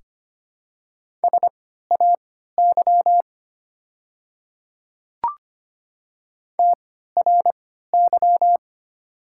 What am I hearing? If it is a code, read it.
Morse code, heard as a keyed tone, about 25 wpm, spaced as for 14 wpm, 700 Hz. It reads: SAY TRY
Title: 3x Character Spacing - Top 200 Words - 25wpm